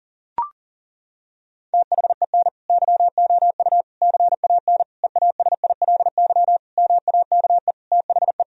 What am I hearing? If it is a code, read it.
Morse code, heard as a keyed tone, about 40 wpm, 700 Hz. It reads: THEN YOU CAN EASILY MAKE THE
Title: Sentences from Top 1000 Words - Review Part 1 - 40wpm